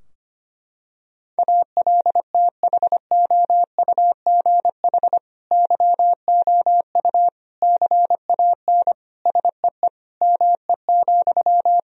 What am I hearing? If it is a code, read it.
Morse code, heard as a keyed tone, about 25 wpm, 700 Hz. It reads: ALTHOUGH YOU CAN SEE ME,
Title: Sentences from Top 1000 Words - Review Part 3 - 25wpm